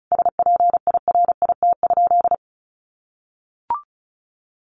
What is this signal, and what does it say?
Morse code, heard as a keyed tone, about 34 wpm, 700 Hz. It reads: SPIRIT?
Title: Sentences from Top 1400 Words 35wpm